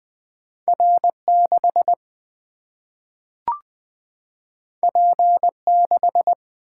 Morse code, heard as a keyed tone, about 20 wpm, 700 Hz. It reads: R 6 P 6
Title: Single-Letter Single-Number 20wpm